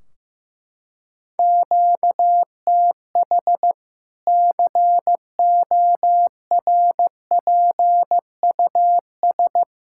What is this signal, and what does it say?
Morse code, heard as a keyed tone, about 15 wpm, 700 Hz. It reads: QTH CORPUS
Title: Practice QSO - Exchange 1 - Beginning - 15wpm